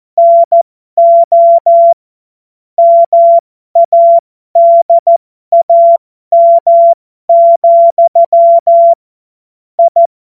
Morse code, heard as a keyed tone, about 14 wpm, spaced as for 12 wpm, 680 Hz. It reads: NO MADAM, I
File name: Morsekurs / Witze